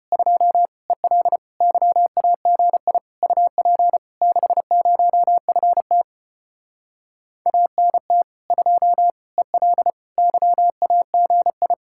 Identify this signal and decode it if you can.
Morse code, heard as a keyed tone, about 33 wpm, 700 Hz. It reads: 2 EL YAGI UP 60FT ANT 2 EL YAGI